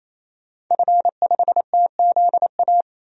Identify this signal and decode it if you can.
Morse code, heard as a keyed tone, about 27 wpm, 700 Hz. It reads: F5TZA
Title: DXCC-100 Call Signs - 5 Characters - 28wpm